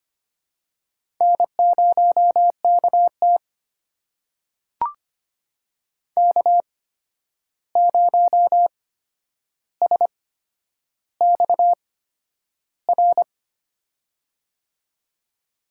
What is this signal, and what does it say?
Morse code, heard as a keyed tone, about 25 wpm, spaced as for 10 wpm, 700 Hz. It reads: N0KT K0SXR
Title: USA Call Signs - 8x Character Spacing - 25wpm